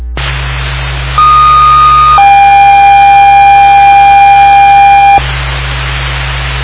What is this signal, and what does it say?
Two-tone page, heard as a one-second tone, then a three-second tone, about 1230 then 800 Hz, with noise. It is heard from 1.2 to 5.2 s.